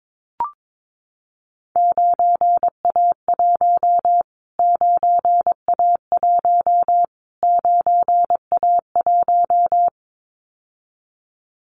Morse code, heard as a keyed tone, about 22 wpm, 700 Hz. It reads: 9A1 9A1 9A1
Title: Callsign Prefixes for Top 25 DXCC - Repeated 3x - 22wpm